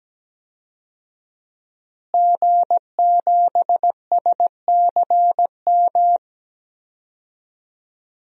Morse code, heard as a keyed tone, about 17 wpm, 700 Hz. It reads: G7SCM
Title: DXCC-100 Call Signs - 5 Characters - Repeated 2x - 17wpm